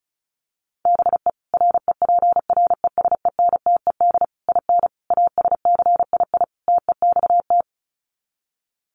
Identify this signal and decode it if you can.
Morse code, heard as a keyed tone, about 34 wpm, 700 Hz. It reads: BE REPRESENTED IN ASCII TEXT
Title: Sets of 5 Words 35wpm